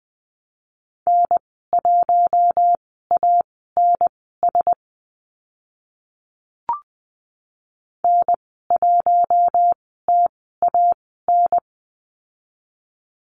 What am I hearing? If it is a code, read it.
Morse code, heard as a keyed tone, about 20 wpm, spaced as for 14 wpm, 700 Hz. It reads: N1ANS N1TAN
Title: 062 Lesson 11g - Letter h - Call Signs 20wpm